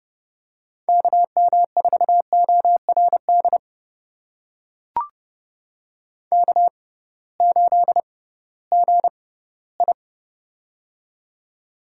Morse code, heard as a keyed tone, about 30 wpm, spaced as for 18 wpm, 700 Hz. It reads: KM4ORD K8GI
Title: USA Call Signs - 6x Character Spacing - 30wpm